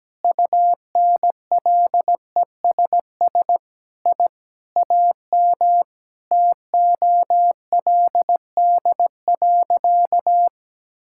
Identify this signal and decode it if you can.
Morse code, heard as a keyed tone, about 17 wpm, 700 Hz. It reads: UNLESS I AM TOLD.
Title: Sentences from Top 1000 Words - Review Part 6 - 17wpm